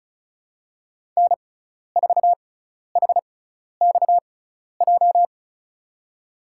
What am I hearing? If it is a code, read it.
Morse code, heard as a keyed tone, about 34 wpm, spaced as for 12 wpm, 700 Hz. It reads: N4HXJ
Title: USA Call Signs - 5 Characters - 6x Character Spacing - 35wpm